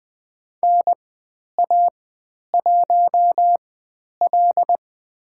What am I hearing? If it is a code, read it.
Morse code, heard as a keyed tone, about 20 wpm, spaced as for 10 wpm, 700 Hz. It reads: NA1L
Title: 095 Lesson 16g - Letter c - Call Signs 20wpm